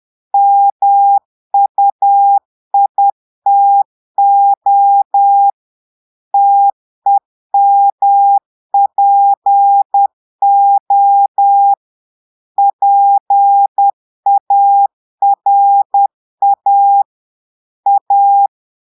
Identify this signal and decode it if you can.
Morse code, heard as a keyed tone, about 10 wpm, 800 Hz. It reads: MUITO TEMPO PARA A
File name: Portugais - texte en clair 1 - vitesse - 10 mots minute